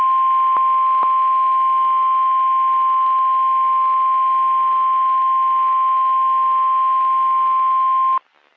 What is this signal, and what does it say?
Emergency Alert System (SAME) tones (the 1050 Hz attention tone). It is heard from 1.0 to 8.2 s.
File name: eas messages / EAS WXM31 Elmira